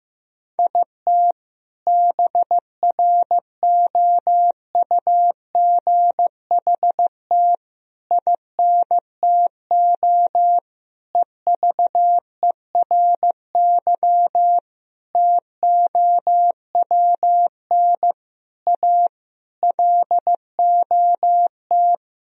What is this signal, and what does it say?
Morse code, heard as a keyed tone, about 15 wpm, 700 Hz. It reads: IT BROUGHT INTO EVERY TOWN A LOT
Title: Sentences from Top 1000 Words - Review Part 3 - 15wpm